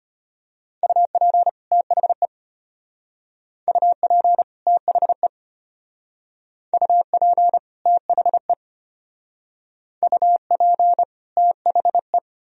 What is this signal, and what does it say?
Morse code, heard as a keyed tone, about 32 wpm, 700 Hz. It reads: UP THE UP THE UP THE UP THE